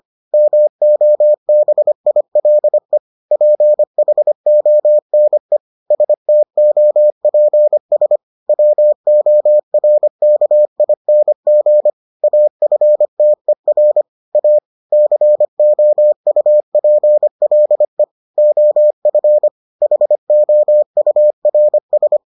Morse code, heard as a keyed tone, about 25 wpm, 600 Hz. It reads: MOBILE PHONE STOPS WORKING AFTER A COUPLE OF HOURS